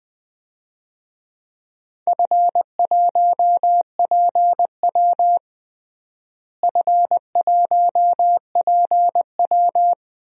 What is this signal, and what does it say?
Morse code, heard as a keyed tone, about 21 wpm, 700 Hz. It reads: F1PW F1PW
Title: DXCC-100 Call Signs - 4 Characters - Repeated 2x - 20wpm